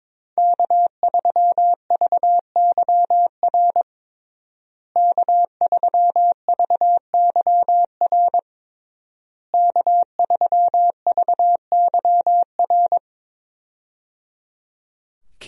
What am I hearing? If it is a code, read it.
Morse code, heard as a keyed tone, about 22 wpm, 690 Hz. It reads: K3VYR K3VYR K3VYR
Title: USA Call Signs - 5 Characters - Repeated 3x - 22wpm